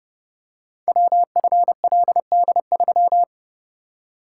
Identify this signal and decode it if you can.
Morse code, heard as a keyed tone, about 30 wpm, 690 Hz. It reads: WFLD3
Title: Code Groups - 5 Letter:Numbers - 30wpm